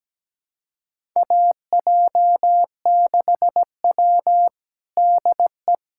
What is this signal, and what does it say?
Morse code, heard as a keyed tone, about 17 wpm, 700 Hz. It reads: AJ6W DE
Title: USA Callsign DE Callsign - 4 Characters - 17wpm